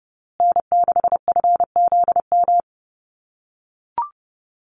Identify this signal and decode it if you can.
Morse code, heard as a keyed tone, about 30 wpm, 700 Hz. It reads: N6FZM
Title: USA Call Signs Repeated 2x 30wpm